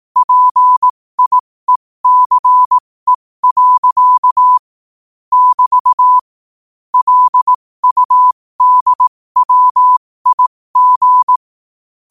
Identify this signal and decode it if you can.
Morse code, heard as a keyed tone, about 18 wpm, spaced as for 15 wpm, 1000 Hz. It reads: PIECE. <BT> LUDWIG
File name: qotdfiles / 15wpm